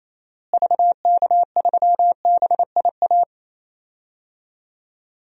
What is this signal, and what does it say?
Morse code, heard as a keyed tone, about 28 wpm, 700 Hz. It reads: VK3BIA
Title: DXCC-100 Call Signs - 6 Characters - Repeated 2x - 28wpm